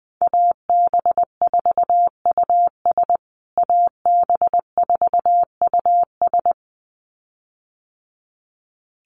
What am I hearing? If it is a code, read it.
Morse code, heard as a keyed tone, about 20 wpm, 700 Hz. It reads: AB4US AB4US
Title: USA Call Signs Repeated 2x 20wpm